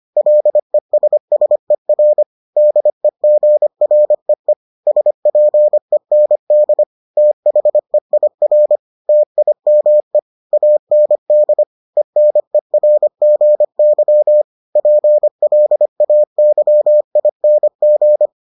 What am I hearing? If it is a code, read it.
Morse code, heard as a keyed tone, about 25 wpm, 600 Hz. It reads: LESSER DEGREE SPEND THEIR TIME AND ENERGY PLAYING